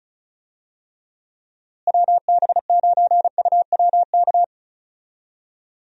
Morse code, heard as a keyed tone, about 36 wpm, 700 Hz. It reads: WB9UWK
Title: USA Call Signs - 6 Characters - Repeated 2x - 35wpm